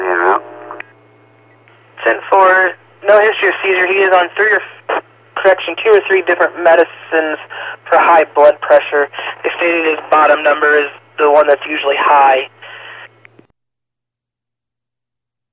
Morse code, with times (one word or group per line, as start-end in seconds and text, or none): none